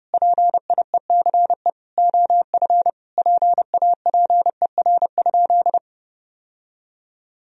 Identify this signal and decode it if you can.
Morse code, heard as a keyed tone, about 30 wpm, 700 Hz. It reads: PIECE OF PAPER?
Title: Sentences from Top 1000 Words - Review Part 2 - 30wpm